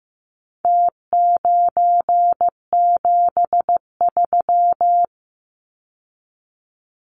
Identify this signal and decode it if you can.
Morse code, heard as a keyed tone, about 15 wpm, 700 Hz. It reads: T973
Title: Numbers - Contest Style - 4-Digit 15wpm